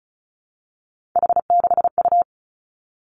Morse code, heard as a keyed tone, about 36 wpm, 700 Hz. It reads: H6U
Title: Three Challenging Letter-Number Combinations 35wpm